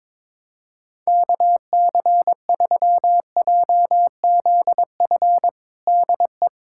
Morse code, heard as a keyed tone, about 22 wpm, 700 Hz. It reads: KC3JZF DE
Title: USA Callsign DE Callsign 22wpm